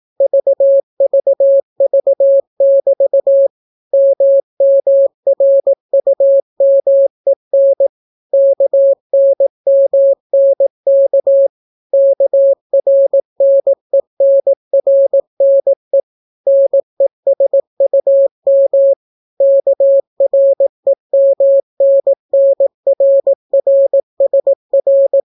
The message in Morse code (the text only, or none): VVV<BT> MMRUMEN KNMNK KRNENRNE NESUM KREMNNRRSR